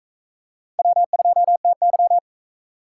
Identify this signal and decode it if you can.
Morse code, heard as a keyed tone, about 40 wpm, 700 Hz. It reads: W2TY